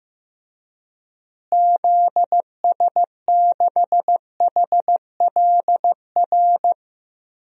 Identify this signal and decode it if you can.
Morse code, heard as a keyed tone, about 15 wpm, 700 Hz. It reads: ZS6HLR
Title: DXCC-100 Call Signs - 6 Characters - Repeated 3x - 15wpm